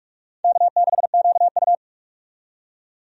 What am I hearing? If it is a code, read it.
Morse code, heard as a keyed tone, about 47 wpm, 700 Hz. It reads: K6QU
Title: USA Call Signs - 4 Characters - 45wpm